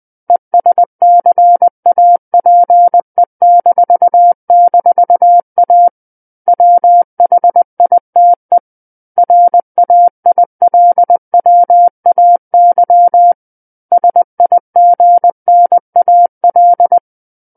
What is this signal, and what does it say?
Morse code, heard as a keyed tone, about 20 wpm, 700 Hz. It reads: ESCAPE--A WHITE RAILWAY SIGNAL
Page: SKCC Learning Center - Morse Code Practice Files